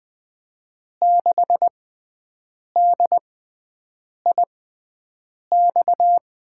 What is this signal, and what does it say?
Morse code, heard as a keyed tone, about 20 wpm, spaced as for 7 wpm, 700 Hz. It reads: 6DIX
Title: USA Call Signs - 6x Character Spacing - 20wpm